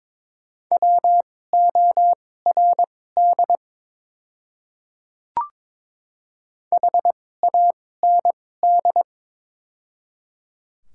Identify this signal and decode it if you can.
Morse code, heard as a keyed tone, about 22 wpm, spaced as for 16 wpm, 700 Hz. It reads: WORD HAND
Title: 2x Character Spacing - Top 200 Words - 22wpm